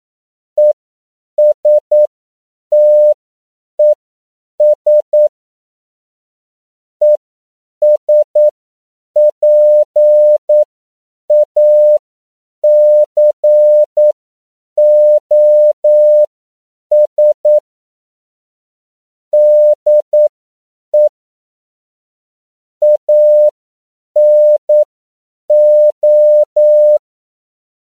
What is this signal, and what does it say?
Morse code, heard as a keyed tone, about 9 wpm, spaced as for 7 wpm, 600 Hz. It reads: ESTES ESPACOS DE ANO
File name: CW / 10PPM